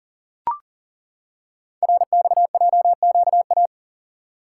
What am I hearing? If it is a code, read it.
Morse code, heard as a keyed tone, about 39 wpm, 700 Hz. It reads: RXJQA
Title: Code Groups - 5 Letters - 40wpm